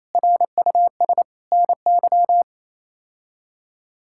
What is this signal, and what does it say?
Morse code, heard as a keyed tone, about 28 wpm, 700 Hz. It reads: RUS NY
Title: K1USN SST QSO Practice - 28wpm